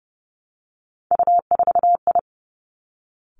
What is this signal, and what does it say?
Morse code, heard as a keyed tone, about 30 wpm, 700 Hz. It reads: U4I
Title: Three Challenging Letter-Number Combinations 30wpm